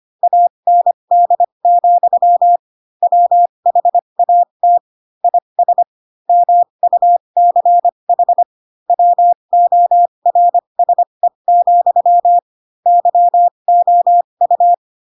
Morse code, heard as a keyed tone, about 25 wpm, spaced as for 22 wpm, 700 Hz. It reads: AND, WHAT IS MUCH WORSE, YOU
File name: Pinocchio / Pinocchio ch4 18 25 ik1hge